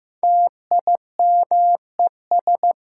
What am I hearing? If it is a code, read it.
Morse code, heard as a keyed tone, about 16 wpm, 700 Hz. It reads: TIMES
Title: Sentences from Top 1000 Words - Review Part 6 - New Word Drill - 15wpm